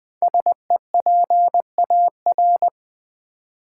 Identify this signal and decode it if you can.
Morse code, heard as a keyed tone, about 20 wpm, 700 Hz. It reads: SEPAR
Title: Build A Word - 5 Letters At A Time - 10-Letter Words - 20wpm